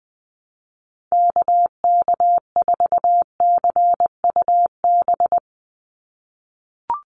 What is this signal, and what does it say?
Morse code, heard as a keyed tone, about 20 wpm, 700 Hz. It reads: KK4CUB
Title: USA Call Signs 20wpm